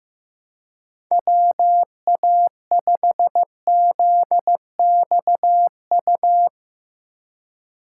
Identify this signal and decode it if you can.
Morse code, heard as a keyed tone, about 16 wpm, 700 Hz. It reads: WA5ZXU
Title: Speed Racing - USA 2x3 Call Signs 15wpm